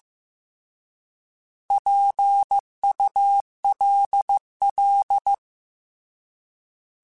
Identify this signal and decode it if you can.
Morse code, heard as a keyed tone, about 15 wpm, 800 Hz. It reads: PULL
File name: Alphabetical / Alphabetical mp3s / Ps @13